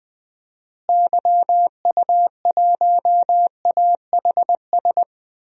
Morse code, heard as a keyed tone, about 20 wpm, 700 Hz. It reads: YU1AHS
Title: DXCC-100 Call Signs - 6 Characters - Repeated 3x - 20wpm